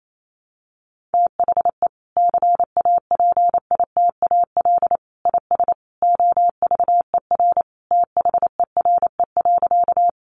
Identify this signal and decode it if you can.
Morse code, heard as a keyed tone, about 28 wpm, 700 Hz. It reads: THE CAPITAL IS OVER THERE.
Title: Sentences from Top 1000 Words 28wpm